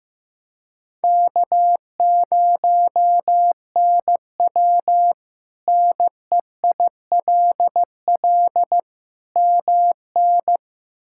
Morse code, heard as a keyed tone, about 15 wpm, 700 Hz. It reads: K0NW NEILL MN
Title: K1USN SST QSO Practice - 15wpm